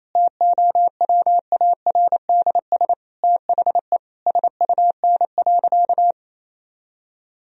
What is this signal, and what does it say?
Morse code, heard as a keyed tone, about 28 wpm, 700 Hz. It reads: TOWARDS THE SUN.
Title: Sentences from Top 1000 Words - Review Part 2 - 28wpm